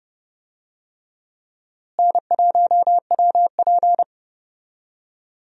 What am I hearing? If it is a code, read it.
Morse code, heard as a keyed tone, about 30 wpm, 700 Hz. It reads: N1WP
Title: USA Call Signs - 4 Characters - Repeated 3x - 30wpm